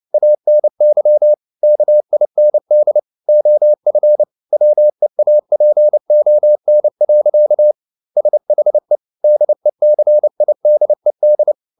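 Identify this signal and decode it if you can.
Morse code, heard as a keyed tone, about 29 wpm, 600 Hz. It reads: ANY KIND OF WEAPON. SHE DECIDED
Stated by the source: Morse code at 29 wpm, sending Ken Follett's novel Never in English, from CW med Gnister.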